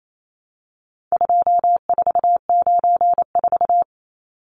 Morse code, heard as a keyed tone, about 28 wpm, 700 Hz. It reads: 2494
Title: Numbers - 4-Digit 28wpm